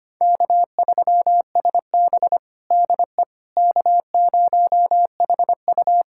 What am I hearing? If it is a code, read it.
Morse code, heard as a keyed tone, about 25 wpm, 700 Hz. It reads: K3SB DE K0HU
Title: USA Callsign DE Callsign - 4 Characters - 25wpm